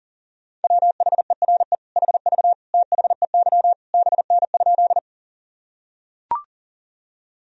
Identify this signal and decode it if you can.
Morse code, heard as a keyed tone, about 37 wpm, 700 Hz. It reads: WHERE HV THEY BN?
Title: Sentences from Top 200 Words and Abbreviations 40wpm